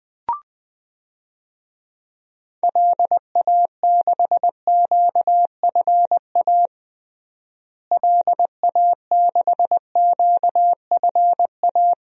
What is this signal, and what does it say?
Morse code, heard as a keyed tone, about 20 wpm, 700 Hz. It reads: LA6QFA LA6QFA
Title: DXCC-100 Call Signs - 6 Characters - Repeated 2x - 20wpm